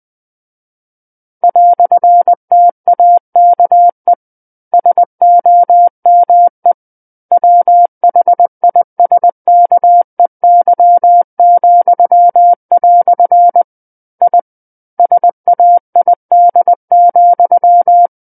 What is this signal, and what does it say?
Morse code, heard as a keyed tone, about 20 wpm, 700 Hz. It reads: "TAKE SOME WHISKEY," I SAID,
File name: 20 WPM